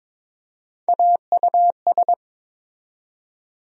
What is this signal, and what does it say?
Morse code, heard as a keyed tone, about 22 wpm, 700 Hz. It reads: AUS
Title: USA Call Sign Postfixes - 3 Letters 22wpm